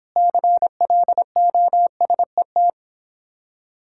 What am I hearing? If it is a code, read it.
Morse code, heard as a keyed tone, about 26 wpm, 700 Hz. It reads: CLOSET